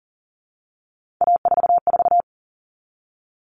Morse code, heard as a keyed tone, about 40 wpm, 700 Hz. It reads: A44
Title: All Cut Numbers - 3-Digit 40wpm